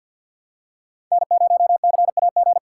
Morse code, heard as a keyed tone, about 47 wpm, 700 Hz. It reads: N0CAG
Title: USA Call Signs - 5 Characters - 3x Character Spacing - 50wpm